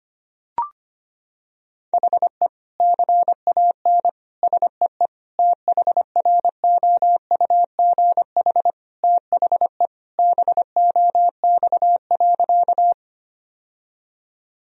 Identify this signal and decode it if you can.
Morse code, heard as a keyed tone, about 25 wpm, 700 Hz. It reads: HE CAN SEE THROUGH THE BOX.
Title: Sentences from Top 1000 Words - Review Part 2 - 25wpm